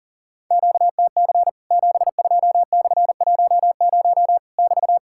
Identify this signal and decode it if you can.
Morse code, heard as a keyed tone, about 40 wpm, 700 Hz. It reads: QTC 72/10 <BT>
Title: Worked All Europe DX Contest - QTC Exchange - 40wpm